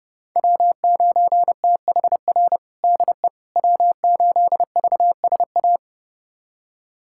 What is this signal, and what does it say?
Morse code, heard as a keyed tone, about 30 wpm, 700 Hz. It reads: W9THR DE W8VSA
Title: USA Callsign DE Callsign - 5 Characters - 30wpm